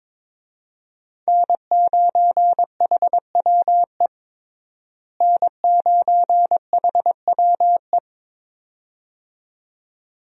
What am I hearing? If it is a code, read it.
Morse code, heard as a keyed tone, about 22 wpm, 700 Hz. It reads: N9HWE N9HWE
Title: USA Call Signs - 5 Characters - Repeated 2x - 22wpm